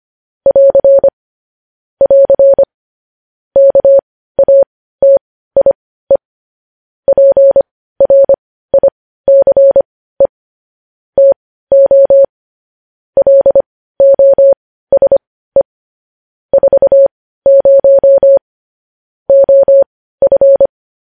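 Morse code, heard as a keyed tone, about 25 wpm, spaced as for 15 wpm, 550 Hz. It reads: <AR> <AR> KATIE PRICE TO LOSE 40 OF
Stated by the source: Morse code sending of news headlines at 15 wpm.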